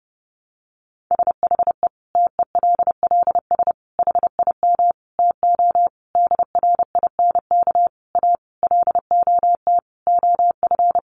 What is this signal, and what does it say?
Morse code, heard as a keyed tone, about 28 wpm, 700 Hz. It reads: SHE TELLS HIM TO DRINK A LOT OF
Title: Additional Easy Phrases 30wpm